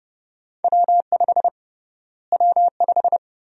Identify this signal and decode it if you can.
Morse code, heard as a keyed tone, about 30 wpm, 700 Hz. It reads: W5 W5
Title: USA Call Sign Prefixes - Repeated 2x - 30wpm